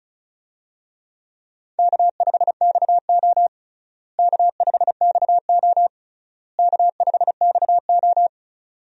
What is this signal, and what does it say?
Morse code, heard as a keyed tone, about 34 wpm, 700 Hz. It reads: K5XO K5XO K5XO
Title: USA Call Signs - 4 Characters - Repeated 3x - 35wpm